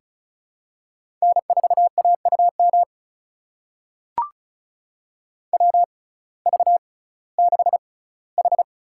Morse code, heard as a keyed tone, about 36 wpm, spaced as for 19 wpm, 700 Hz. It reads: N4AUM WV6H
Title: USA Call Signs - 5 Characters - 6x Character Spacing - 35wpm